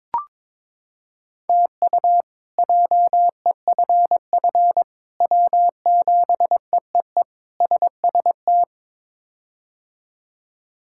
Morse code, heard as a keyed tone, about 22 wpm, 700 Hz. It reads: TU JEFF W7EEE SST
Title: K1USN SST QSO Practice - 22wpm